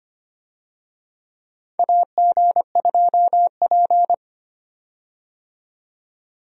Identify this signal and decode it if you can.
Morse code, heard as a keyed tone, about 25 wpm, 700 Hz. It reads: AG2P
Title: USA Call Signs - 4 Characters - 25wpm